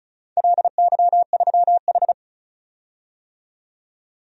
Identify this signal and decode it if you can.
Morse code, heard as a keyed tone, about 36 wpm, 700 Hz. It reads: LY3H
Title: DXCC-100 Call Signs - 4 Characters - 35wpm